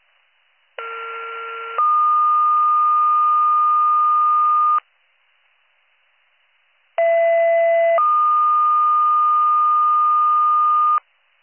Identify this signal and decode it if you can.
Two-tone page, heard as a one-second tone, then a three-second tone, about 1370 then 1220 Hz, about 670 then 1220 Hz. It is heard from 0.8 to 11.0 s.